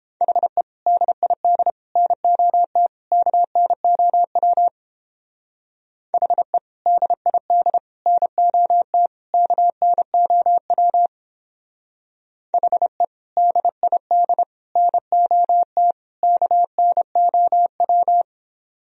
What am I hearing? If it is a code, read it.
Morse code, heard as a keyed tone, about 29 wpm, 700 Hz. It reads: HE DID NOT KNOW HE DID NOT KNOW HE DID NOT KNOW